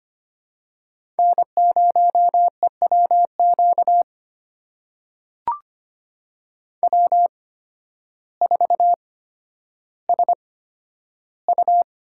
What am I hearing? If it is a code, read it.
Morse code, heard as a keyed tone, about 25 wpm, spaced as for 12 wpm, 700 Hz. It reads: N0EWQ W4SU
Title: USA Call Signs - 5 Characters - 8x Character Spacing - 25wpm